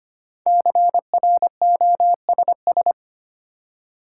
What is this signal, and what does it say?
Morse code, heard as a keyed tone, about 25 wpm, 700 Hz. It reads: CROSS
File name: 5 Letter Words - First 3 Letters Given - 25wpm